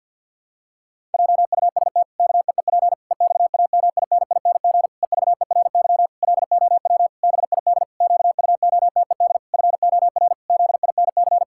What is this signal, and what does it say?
Morse code, heard as a keyed tone, about 50 wpm, 700 Hz. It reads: JUST KEEP EXAMINING EVERY LOW BID QUOTED FOR ZINC
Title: Pangram Challenge v2 - 50wpm